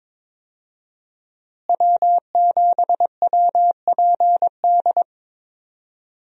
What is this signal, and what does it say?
Morse code, heard as a keyed tone, about 22 wpm, 700 Hz. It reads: W7WPD
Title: USA Call Signs - 5 Characters - Repeated 2x - 22wpm